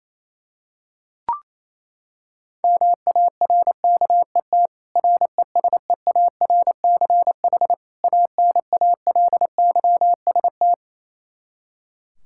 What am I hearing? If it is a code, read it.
Morse code, heard as a keyed tone, about 28 wpm, 700 Hz. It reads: MARKET RESEARCH ANALYST
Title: Practice QSO - Exchange 3 - More Info - 28wpm